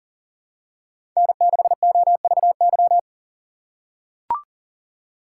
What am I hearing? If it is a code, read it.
Morse code, heard as a keyed tone, about 40 wpm, 700 Hz. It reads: N6OVY
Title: USA Call Signs - 5 Characters - 3x Character Spacing - 40wpm